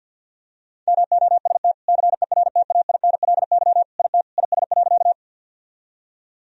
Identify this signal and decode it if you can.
Morse code, heard as a keyed tone, about 50 wpm, 700 Hz. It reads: MOST CERTAINLY IT IS.
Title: Sentences from Top 1000 Words - Review Part 1 - 50wpm